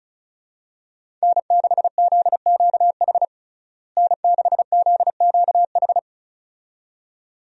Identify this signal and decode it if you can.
Morse code, heard as a keyed tone, about 34 wpm, 700 Hz. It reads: N6ZQH N6ZQH
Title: USA Call Signs - 5 Characters - Repeated 2x - 35wpm